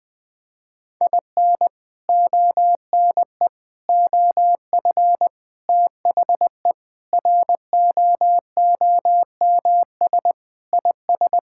Morse code, heard as a keyed tone, about 19 wpm, 700 Hz. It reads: IN ONE OF THE ROOMS IS
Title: Sentences with 4-Word Phrases - Spoken Beginning - 20wpm